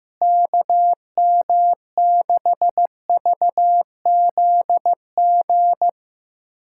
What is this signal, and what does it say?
Morse code, heard as a keyed tone, about 15 wpm, 700 Hz. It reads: KM6VZG
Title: USA Callsign DE Callsign 15wpm